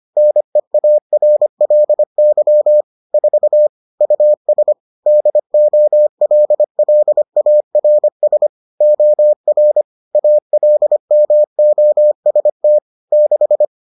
Morse code, heard as a keyed tone, about 25 wpm, 600 Hz. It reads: NEARLY 4 US DOLLARS OR ALMOST 6